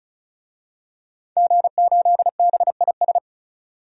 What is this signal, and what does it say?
Morse code, heard as a keyed tone, about 36 wpm, 700 Hz. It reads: G8BIS
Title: DXCC-100 Call Signs - 5 Characters - 35wpm